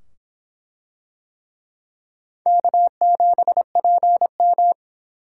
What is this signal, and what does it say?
Morse code, heard as a keyed tone, about 23 wpm, 700 Hz. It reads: K7PM